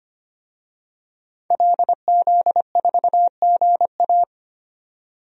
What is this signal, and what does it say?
Morse code, heard as a keyed tone, about 25 wpm, 700 Hz. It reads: LZ4GA
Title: DXCC-100 Call Signs - 5 Characters - 25wpm